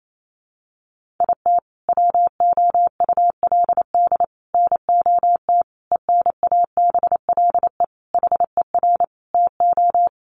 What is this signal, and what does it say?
Morse code, heard as a keyed tone, about 28 wpm, 700 Hz. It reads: IT WOULD NOT ENABLE HER TO
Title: Sentences from Top 1500 Words 28wpm